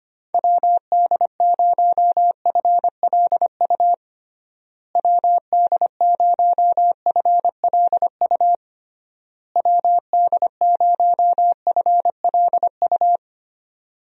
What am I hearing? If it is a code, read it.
Morse code, heard as a keyed tone, about 25 wpm, 700 Hz. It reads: WD0FLU WD0FLU WD0FLU
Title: USA Call Signs - 6 Characters - Repeated 3x - 25wpm